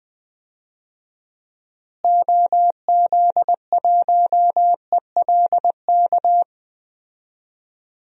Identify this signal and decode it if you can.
Morse code, heard as a keyed tone, about 21 wpm, 700 Hz. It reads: OZ1ELK
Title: DXCC-100 Call Signs - 6 Characters - Repeated 2x - 20wpm